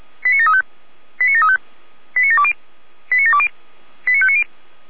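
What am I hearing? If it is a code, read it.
Five-tone selective call (ZVEI), heard as quick runs of five tones, tones of about 70 ms. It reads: 89735 89735 89720 89720 89590
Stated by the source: ZVEI five-tone selective call 89706 received on SDR.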